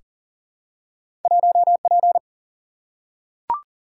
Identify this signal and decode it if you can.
Morse code, heard as a keyed tone, about 42 wpm, 700 Hz. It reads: 1P
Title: Single-Letter Single-Number 40wpm